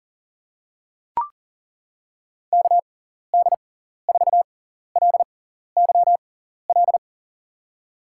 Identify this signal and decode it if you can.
Morse code, heard as a keyed tone, about 40 wpm, spaced as for 14 wpm, 700 Hz. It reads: KD4LYL
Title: USA Call Signs - 6 Characters - 6x Character Spacing - 40wpm